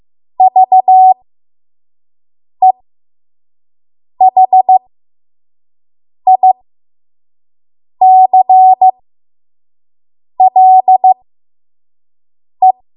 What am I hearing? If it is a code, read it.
Morse code, heard as a keyed tone, about 15 wpm, spaced as for 5 wpm, 750 Hz. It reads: VEHICLE
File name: Morse / 05WPM